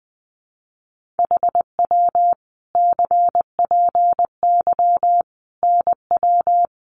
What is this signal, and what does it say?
Morse code, heard as a keyed tone, about 20 wpm, 700 Hz. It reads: HW CPY NW
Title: Common QSO Phrases 20wpm